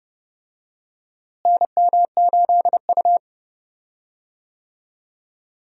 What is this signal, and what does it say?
Morse code, heard as a keyed tone, about 29 wpm, 700 Hz. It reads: NM8U
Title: USA Call Signs - 4 Characters - Repeated 2x - 30wpm